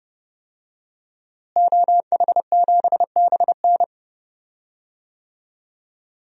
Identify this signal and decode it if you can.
Morse code, heard as a keyed tone, about 31 wpm, 700 Hz. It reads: OH7BN
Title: DXCC-100 Call Signs - 5 Characters - Repeated 2x - 30wpm